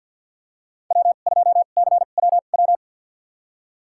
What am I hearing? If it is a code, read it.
Morse code, heard as a keyed tone, about 50 wpm, spaced as for 36 wpm, 690 Hz. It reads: W2CWW
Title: USA Call Signs - 5 Characters - 2x Character Spacing - 50wpm